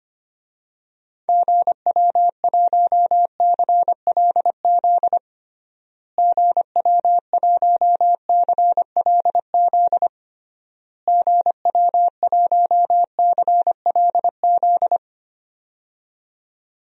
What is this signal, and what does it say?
Morse code, heard as a keyed tone, about 25 wpm, 700 Hz. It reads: GW1CLZ GW1CLZ GW1CLZ
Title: DXCC-100 Call Signs - 6 Characters - Repeated 3x - 25wpm